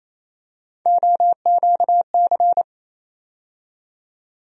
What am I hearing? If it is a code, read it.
Morse code, heard as a keyed tone, about 28 wpm, 700 Hz. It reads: OQC
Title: USA Call Sign Postfixes - 3 Letters 28wpm